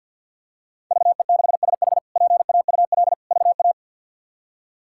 Morse code, heard as a keyed tone, about 47 wpm, 700 Hz. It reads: VE6SH PAUL VA
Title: K1USN SST QSO Practice - 50wpm